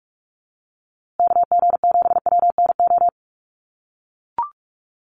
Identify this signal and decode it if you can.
Morse code, heard as a keyed tone, about 44 wpm, 700 Hz. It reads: KG7WNO
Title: USA Call Signs Repeated 2x 45wpm